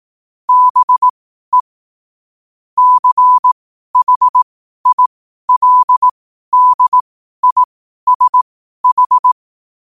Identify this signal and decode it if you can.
Morse code, heard as a keyed tone, about 18 wpm, spaced as for 12 wpm, 1000 Hz. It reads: BE CHILDISH
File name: qotdfiles / 12wpm